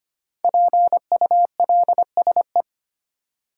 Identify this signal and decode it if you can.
Morse code, heard as a keyed tone, about 25 wpm, 700 Hz. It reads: PULSE
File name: 5 Letter Words - First 3 Letters Given - 25wpm